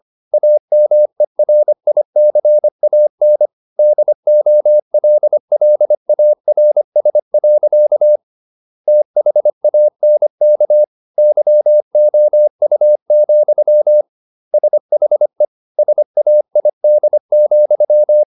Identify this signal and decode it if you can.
Morse code, heard as a keyed tone, about 25 wpm, 600 Hz. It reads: AMERICAN DOLLARS. THANK YOU, SHE SAID,